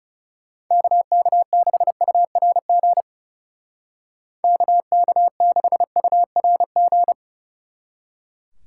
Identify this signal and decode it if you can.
Morse code, heard as a keyed tone, about 36 wpm, 700 Hz. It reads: KK6URG KK6URG
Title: Speed Racing - USA 2x3 Call Signs 30wpm